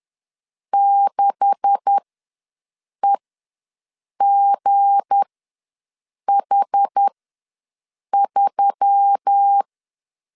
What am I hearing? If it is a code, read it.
Morse code, heard as a keyed tone, about 10 wpm, spaced as for 6 wpm, 800 Hz. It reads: 6EGH3
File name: LEZIONI / 9 286 ws 10wpm 4032